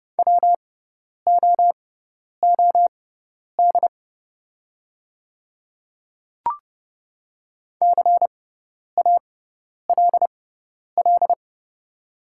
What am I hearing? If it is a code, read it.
Morse code, heard as a keyed tone, about 30 wpm, spaced as for 10 wpm, 700 Hz. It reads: WOOD CALL
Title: 6x Character Spacing - Top 300 Words - 30wpm